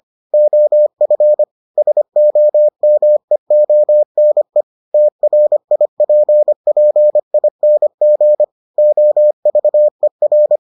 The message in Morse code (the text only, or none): OF SOMEONE TRIPPING OVER